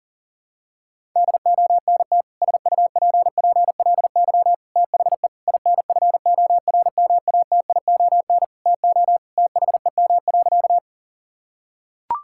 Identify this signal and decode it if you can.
Morse code, heard as a keyed tone, about 37 wpm, 700 Hz. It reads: DONT SUPPLY THE INFORMATION TO THEM.
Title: Sentences from Top 1000 Words - Review Part 5 - 40wpm